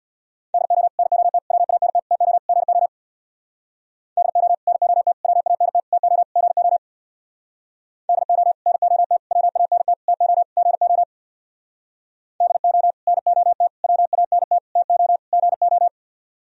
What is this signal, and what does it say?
Morse code, heard as a keyed tone, about 63 wpm, 700 Hz. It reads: DO NOT WANT TO GO DO NOT WANT TO GO DO NOT WANT TO GO DO NOT WANT TO GO